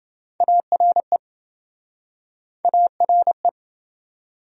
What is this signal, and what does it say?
Morse code, heard as a keyed tone, about 29 wpm, 700 Hz. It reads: ARE ARE